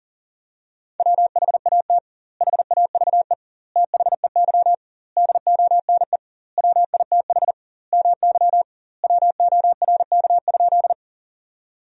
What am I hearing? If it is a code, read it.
Morse code, heard as a keyed tone, about 40 wpm, 700 Hz. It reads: WHAT HAVE THEY DONE WITH MY WORK?
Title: 2x Word-Spacing - Sentences from Top 500 Words - 40wpm